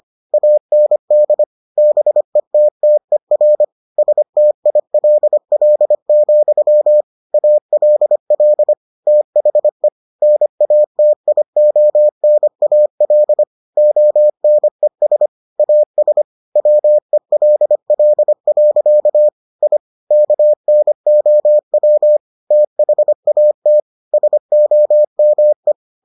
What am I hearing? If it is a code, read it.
Morse code, heard as a keyed tone, about 25 wpm, 600 Hz. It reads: AND BETTER STILL, ALL THE NATIONAL ONES AS WELL. I KNOW THAT SOME